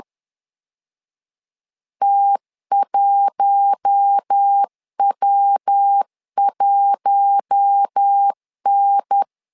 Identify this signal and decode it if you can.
Morse code, heard as a keyed tone, about 10 wpm, 800 Hz. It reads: T1W1N
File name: LEZIONI / 4 Q1 10wpm 2448